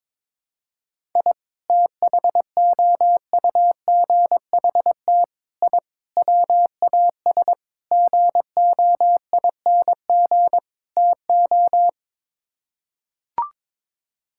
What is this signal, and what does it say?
Morse code, heard as a keyed tone, about 22 wpm, 700 Hz. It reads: I THOUGHT I WAS GOING TO
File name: practice / 22wpm / Sets of 6 Words - 22wpm